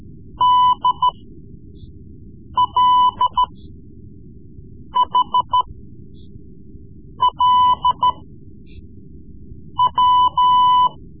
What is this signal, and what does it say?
Morse code, heard as a keyed tone, about 13 wpm, spaced as for 5 wpm, 1000 Hz. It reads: DLHLW
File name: Morse Lessons / Morse Lessons / Tape-3B